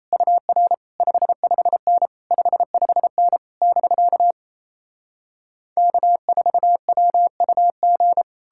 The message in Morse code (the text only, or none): UR 55N 55N <BK> K4WUG